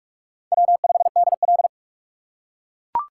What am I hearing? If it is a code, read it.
Morse code, heard as a keyed tone, about 44 wpm, 700 Hz. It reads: W5DL